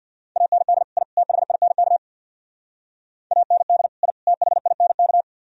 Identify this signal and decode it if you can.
Morse code, heard as a keyed tone, about 55 wpm, 700 Hz. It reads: AND I THINK AND I THINK
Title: Speed-Racing - Sets of 3 Words 50wpm